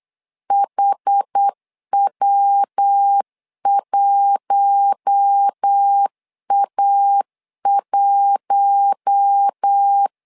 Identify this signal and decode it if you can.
Morse code, heard as a keyed tone, about 8 wpm, 800 Hz. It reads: HW1A1
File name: LEZIONI / 4 Q1 8wpm 3035